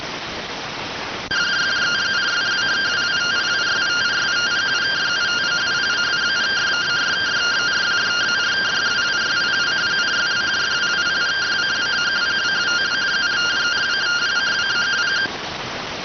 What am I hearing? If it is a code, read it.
Two-tone teleprinter signal, about 170 Hz shift.